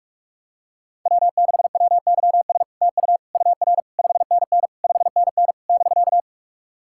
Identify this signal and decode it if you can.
Morse code, heard as a keyed tone, about 46 wpm, 700 Hz. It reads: W6WYS TU UR 5NN 5NN <BK>
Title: SOTA QSO Practice 45wpm